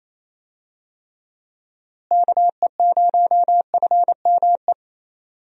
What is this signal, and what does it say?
Morse code, heard as a keyed tone, about 27 wpm, 700 Hz. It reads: KE0FME
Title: USA Call Signs - 6 Characters - Repeated 3x - 28wpm